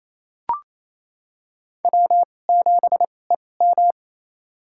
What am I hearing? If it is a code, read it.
Morse code, heard as a keyed tone, about 28 wpm, spaced as for 20 wpm, 700 Hz. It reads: W7EM
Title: USA Call Signs - 2x Character Spacing - 28wpm